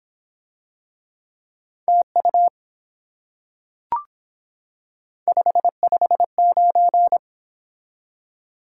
Morse code, heard as a keyed tone, about 24 wpm, 700 Hz. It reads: TU 559